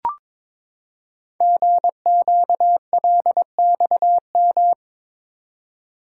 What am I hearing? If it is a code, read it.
Morse code, heard as a keyed tone, about 22 wpm, 700 Hz. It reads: GQLXM
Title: Code Groups - 5 Letters - 22wpm